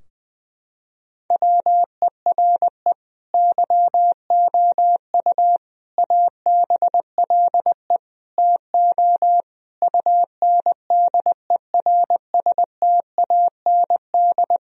Morse code, heard as a keyed tone, about 20 wpm, 700 Hz. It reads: WERE YOU ABLE TO UNDERSTAND
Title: Sentences from Top 1000 Words - Review Part 2 - 20wpm